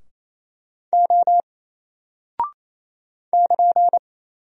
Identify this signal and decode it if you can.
Morse code, heard as a keyed tone, about 28 wpm, 700 Hz. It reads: O <KN>
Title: Single Character (Letter:Number:Punctuation:Prosign) Alt 28wpm